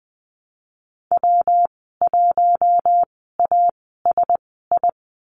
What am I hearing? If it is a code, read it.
Morse code, heard as a keyed tone, about 19 wpm, spaced as for 14 wpm, 700 Hz. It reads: W1ASI
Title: 167 Lesson 27g - Number 7 - Call Signs 20wpm